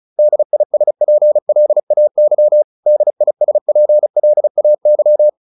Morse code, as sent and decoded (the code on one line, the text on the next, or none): -.. .. ... .--. .-.. .- -.-- / -.. .. ... .--. .-.. .- -.--
DISPLAY DISPLAY